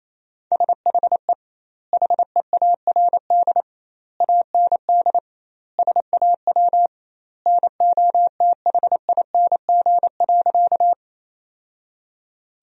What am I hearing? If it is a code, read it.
Morse code, heard as a keyed tone, about 28 wpm, 700 Hz. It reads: SHE HEARD AND SAW NOTHING.
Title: 2x Word-Spacing - Sentences from Top 500 Words - 28wpm